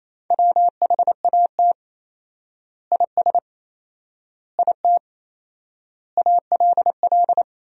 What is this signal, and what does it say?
Morse code, heard as a keyed tone, about 28 wpm, 700 Hz. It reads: WHAT IS IT ALL
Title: 4x Word-Spacing - Sentences from Top 100 Words - 28wpm